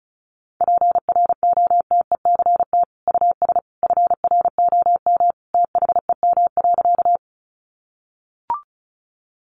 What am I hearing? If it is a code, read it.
Morse code, heard as a keyed tone, about 34 wpm, 700 Hz. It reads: PROTECT US FROM THEM.
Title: Sentences from Top 900 Words 35wpm